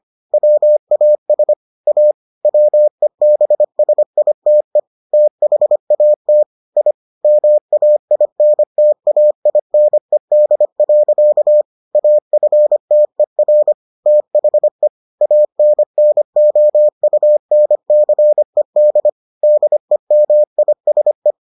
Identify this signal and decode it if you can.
Morse code, heard as a keyed tone, about 25 wpm, 600 Hz. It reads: WAS A WEBSITE THAT I MAINTAINED. AFTER THE ANNOUNCED DEMISE